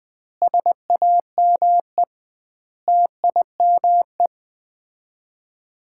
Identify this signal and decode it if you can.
Morse code, heard as a keyed tone, about 21 wpm, 700 Hz. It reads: SAME TIME
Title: 2x Word-Spacing - Sets of 5 Words - 20wpm